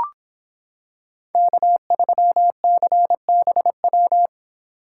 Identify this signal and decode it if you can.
Morse code, heard as a keyed tone, about 26 wpm, 700 Hz. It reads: K3CBW